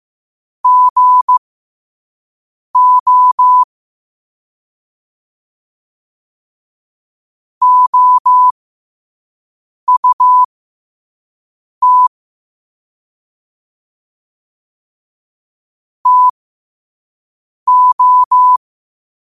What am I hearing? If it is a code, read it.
Morse code, heard as a keyed tone, about 15 wpm, spaced as for 5 wpm, 1000 Hz. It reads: GO OUT TO
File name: qotdfiles / 5wpm